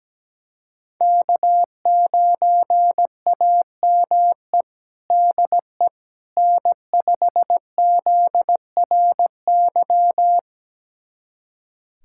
Morse code, heard as a keyed tone, about 17 wpm, 700 Hz. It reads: K9AME DE N5ZRY
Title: USA Callsign DE Callsign - 5 Characters - 17wpm